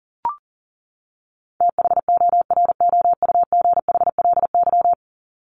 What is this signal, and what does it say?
Morse code, heard as a keyed tone, about 40 wpm, 700 Hz. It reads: THOROUGHLY
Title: LY Suffix - Top 100 Words with Base Words 40wpm